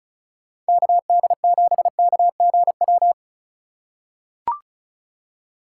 Morse code, heard as a keyed tone, about 35 wpm, 700 Hz. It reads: KD7KGW
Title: USA Call Signs - 6 Characters - 6x Character Spacing - 35wpm